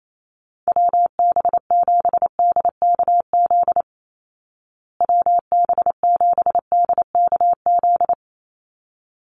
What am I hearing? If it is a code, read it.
Morse code, heard as a keyed tone, about 28 wpm, 700 Hz. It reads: WB7DKZ WB7DKZ
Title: Call Signs Repeated 3x - 28wpm